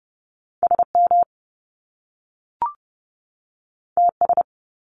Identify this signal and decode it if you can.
Morse code, heard as a keyed tone, about 30 wpm, 700 Hz. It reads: SM TS
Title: Top Two Letter Combinations in Words 30wpm